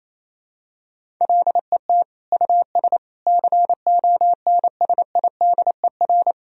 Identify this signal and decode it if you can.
Morse code, heard as a keyed tone, about 28 wpm, 700 Hz. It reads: LET US CONSIDER
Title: Sentences from Top 1000 Words - Review Part 4 - 28wpm